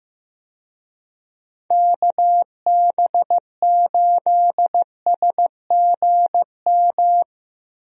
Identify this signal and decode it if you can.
Morse code, heard as a keyed tone, about 15 wpm, 700 Hz. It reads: KB8SGM
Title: USA Call Signs - 6 Characters - Repeated 3x - 15wpm